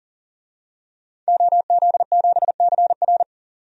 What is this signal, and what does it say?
Morse code, heard as a keyed tone, about 37 wpm, 700 Hz. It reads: OZ7CR
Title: DXCC-100 Call Signs - 5 Characters - 40wpm